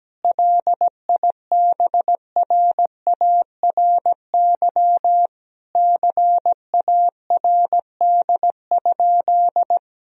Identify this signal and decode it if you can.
Morse code, heard as a keyed tone, about 17 wpm, 700 Hz. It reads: LIBRARY CARD?
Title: Sentences from Top 1000 Words - Review Part 4 - 17wpm